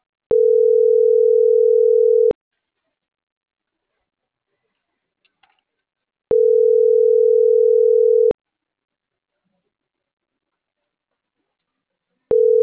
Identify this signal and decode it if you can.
Ringback tone (North America), 440 + 480 Hz.
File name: CENTRAL NMP / out marianacruznmp